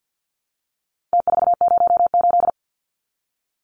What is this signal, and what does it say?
Morse code, heard as a keyed tone, about 48 wpm, 700 Hz. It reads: T408
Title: Numbers - Contest Style - 4-Digit 50wpm